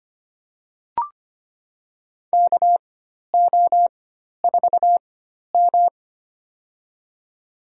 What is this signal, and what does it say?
Morse code, heard as a keyed tone, about 25 wpm, spaced as for 12 wpm, 700 Hz. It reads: KO4M
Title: USA Call Signs - 4 Characters - 4x Character Spacing - 25wpm